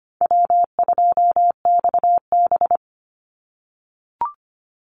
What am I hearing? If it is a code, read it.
Morse code, heard as a keyed tone, about 25 wpm, 700 Hz. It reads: W2XB
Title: Easy Call Signs - 25wpm